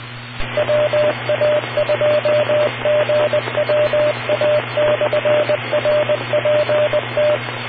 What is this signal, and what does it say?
Morse code, heard as a keyed tone, about 20 wpm, 600 Hz, with radio noise. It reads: WA2GWA/RPT